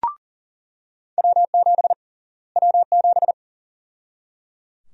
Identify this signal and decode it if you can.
Morse code, heard as a keyed tone, about 37 wpm, 700 Hz. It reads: W7 W7
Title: USA Call Sign Prefixes - Repeated 2x - 40wpm